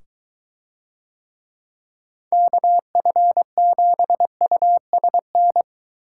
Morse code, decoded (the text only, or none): KF7USN